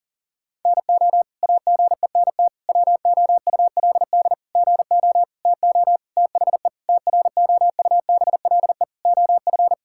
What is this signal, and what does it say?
Morse code, heard as a keyed tone, about 40 wpm, 700 Hz. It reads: NO AGENT WOULD GO TO THE TROUBLE OF
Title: Sentences from Top 1600 Words 40wpm